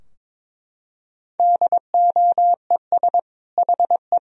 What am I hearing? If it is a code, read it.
Morse code, heard as a keyed tone, about 22 wpm, 700 Hz. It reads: DOES HE
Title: Sentences from Top 1000 Words - Review Part 2 - 22wpm